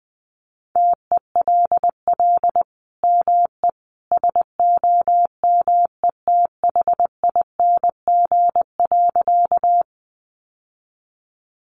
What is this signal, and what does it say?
Morse code, heard as a keyed tone, about 20 wpm, 700 Hz. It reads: TELL ME SOMETHING.
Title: Sentences from Top 600 Words 20wpm